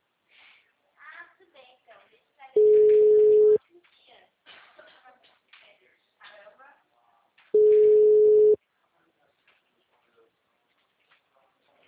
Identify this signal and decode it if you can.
Ringback tone (Europe, 425 Hz), with speech.